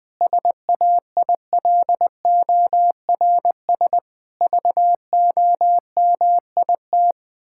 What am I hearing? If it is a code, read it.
Morse code, heard as a keyed tone, about 20 wpm, 700 Hz. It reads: SAILORS VOMIT
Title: Pangram Challenge v2 - 20wpm